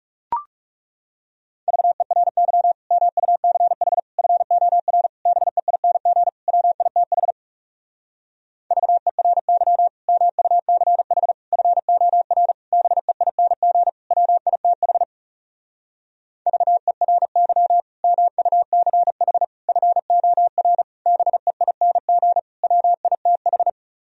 Morse code, heard as a keyed tone, about 40 wpm, 700 Hz. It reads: VERY MUCH FOR BEING WITH VERY MUCH FOR BEING WITH VERY MUCH FOR BEING WITH